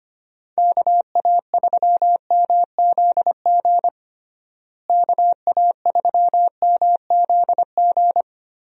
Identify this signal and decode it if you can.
Morse code, heard as a keyed tone, about 25 wpm, 700 Hz. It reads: KA3MZG KA3MZG
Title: USA Call Signs - 6 Characters - Repeated 3x - 25wpm